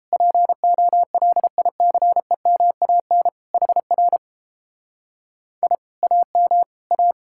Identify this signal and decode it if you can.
Morse code, heard as a keyed tone, about 29 wpm, 700 Hz. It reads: POLICEMAN HR I AM A